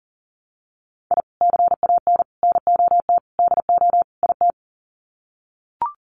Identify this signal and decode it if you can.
Morse code, heard as a keyed tone, about 40 wpm, 700 Hz. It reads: I CAN NOT DO IT
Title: Sets of 5 Words Encore 40wpm